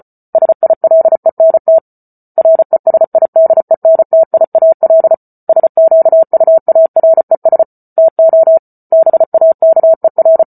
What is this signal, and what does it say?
Morse code, heard as a keyed tone, about 34 wpm, 650 Hz. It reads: SILENT RESIDENTIAL SQUARES TO BAKER